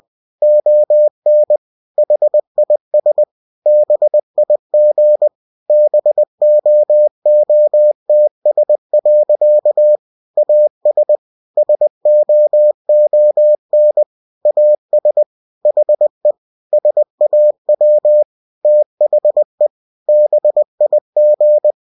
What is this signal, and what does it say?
Morse code, heard as a keyed tone, about 20 wpm, 600 Hz. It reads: ON HIS BIG BOOTS. AS SOON AS HE SAW THE BIG
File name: cw-books / winnie-the-pooh / winnie-20wpm-page-0076